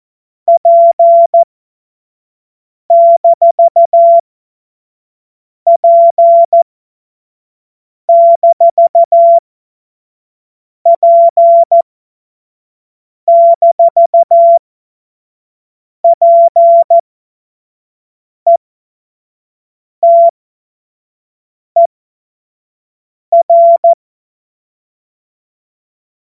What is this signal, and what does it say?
Morse code, heard as a keyed tone, about 14 wpm, spaced as for 5 wpm, 680 Hz. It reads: P-P-P-PETER
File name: Morsekurs / Witze / Witz 31.Oktober 2023 25 F70